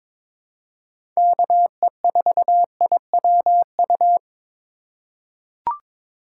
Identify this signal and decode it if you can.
Morse code, heard as a keyed tone, about 22 wpm, 700 Hz. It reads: KE4IWU
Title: USA Call Signs - 6 Characters - 3x Character Spacing - 22wpm